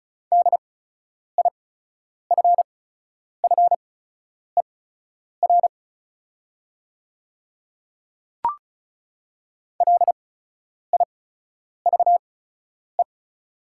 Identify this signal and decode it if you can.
Morse code, heard as a keyed tone, about 34 wpm, spaced as for 10 wpm, 690 Hz. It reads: DIFFER LIVE
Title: 8x Character Spacing - Top 200 Words - 35wpm